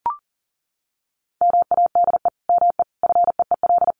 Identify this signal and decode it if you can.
Morse code, heard as a keyed tone, about 40 wpm, 700 Hz. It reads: MADE ME FEEL
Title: Sentences from Top 400 Words 40wpm